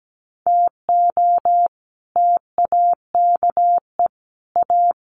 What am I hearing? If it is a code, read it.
Morse code, heard as a keyed tone, about 17 wpm, 700 Hz. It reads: TO TAKE A
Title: Sets of 5 Words 17wpm